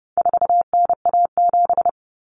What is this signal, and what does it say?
Morse code, heard as a keyed tone, about 30 wpm, 700 Hz. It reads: 4NA7
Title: Common Cut Numbers - 4-Digit 30wpm